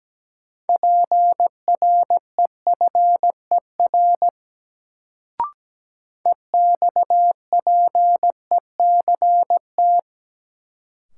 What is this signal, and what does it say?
Morse code, heard as a keyed tone, about 17 wpm, 700 Hz. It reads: PREFER EXPECT
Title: Sentences from Top 1100 Words - New Word Drill Alt - 17wpm